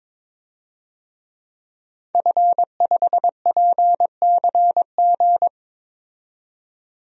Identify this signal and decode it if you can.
Morse code, heard as a keyed tone, about 23 wpm, 700 Hz. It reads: F5PCG
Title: DXCC-100 Call Signs - 5 Characters - 22wpm